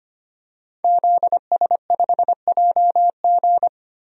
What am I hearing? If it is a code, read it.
Morse code, heard as a keyed tone, about 25 wpm, 700 Hz. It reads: ZS5JG
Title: DXCC-100 Call Signs - 5 Characters - Repeated 3x - 25wpm